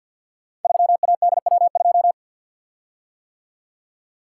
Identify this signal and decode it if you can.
Morse code, heard as a keyed tone, about 46 wpm, 700 Hz. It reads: 3ADW2
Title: Code Groups - 5 Letter:Numbers - 50wpm